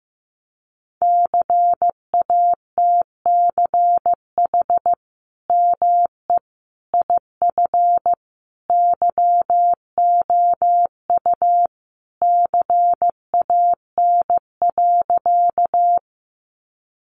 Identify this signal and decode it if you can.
Morse code, heard as a keyed tone, about 15 wpm, 700 Hz. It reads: CATCH ME IF YOU CAN.
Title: Sentences from Top 1500 Words 15wpm